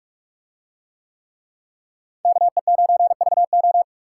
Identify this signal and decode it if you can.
Morse code, heard as a keyed tone, about 46 wpm, 700 Hz. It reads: KE9VO